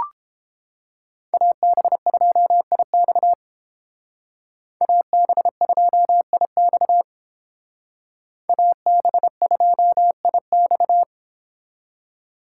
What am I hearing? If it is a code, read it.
Morse code, heard as a keyed tone, about 30 wpm, 700 Hz. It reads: AB2IX AB2IX AB2IX